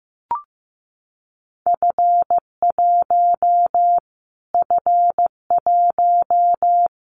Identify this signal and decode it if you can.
Morse code, heard as a keyed tone, about 15 wpm, 700 Hz. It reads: F1 F1
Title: Callsign Prefixes for Top 25 DXCC - Repeated 3x - 15wpm